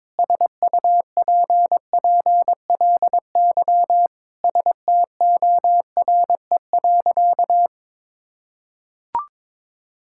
Morse code, heard as a keyed tone, about 22 wpm, 700 Hz. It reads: SUPPLY STORE.
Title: Sentences from Top 1000 Words - Review Part 5 - 22wpm